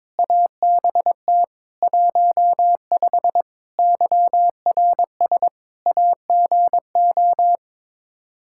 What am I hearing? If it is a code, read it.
Morse code, heard as a keyed tone, about 22 wpm, 700 Hz. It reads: ABT 15 YRS AGO
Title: Practice QSO - Exchange 3 - More Info - 22wpm